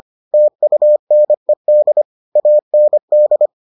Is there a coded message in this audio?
Yes: Morse code from 0.3 to 3.5 s, heard as a keyed tone, about 25 wpm, 600 Hz. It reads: TUNED AND